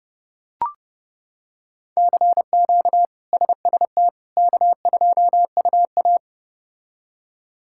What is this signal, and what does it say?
Morse code, heard as a keyed tone, about 30 wpm, 700 Hz. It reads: CQ SST K2UA
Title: K1USN SST QSO Practice - 30wpm